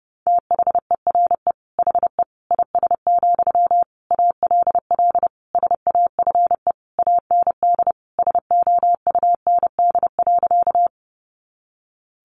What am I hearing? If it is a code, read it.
Morse code, heard as a keyed tone, about 30 wpm, 700 Hz. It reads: THERE HE IS, ALL SAFE AND SOUND.
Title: Sentences from Top 900 Words 30wpm